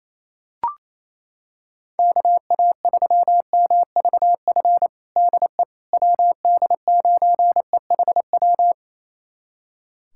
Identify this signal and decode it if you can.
Morse code, heard as a keyed tone, about 27 wpm, 700 Hz. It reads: KA3MVF DE WD9EHW
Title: USA Callsign DE Callsign 28wpm